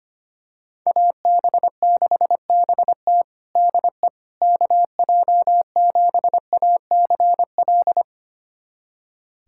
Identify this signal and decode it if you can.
Morse code, heard as a keyed tone, about 25 wpm, 700 Hz. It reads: AB6BT DE KJ7ACL
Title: USA Callsign DE Callsign 25wpm